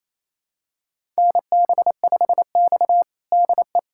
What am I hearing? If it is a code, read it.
Morse code, heard as a keyed tone, about 28 wpm, 700 Hz. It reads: NB5X DE
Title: USA Callsign DE Callsign - 4 Characters - 28wpm